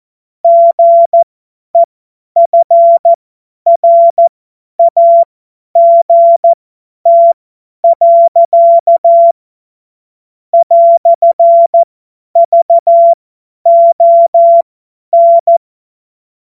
Morse code, heard as a keyed tone, about 14 wpm, spaced as for 10 wpm, 680 Hz. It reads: GEFRAGT. "VON
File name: Morsekurs / Witze / Witz 27. Januar 2026 50 F70